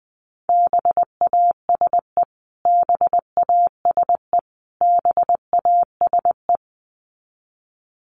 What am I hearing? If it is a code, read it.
Morse code, heard as a keyed tone, about 20 wpm, 700 Hz. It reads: BASE BASE BASE
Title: Top 500 Words Repeated 20wpm